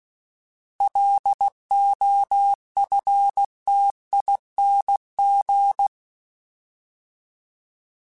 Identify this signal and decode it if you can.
Morse code, heard as a keyed tone, about 16 wpm, 800 Hz. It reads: LOFTING
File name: Alphabetical / Alphabetical mp3s / Ls @15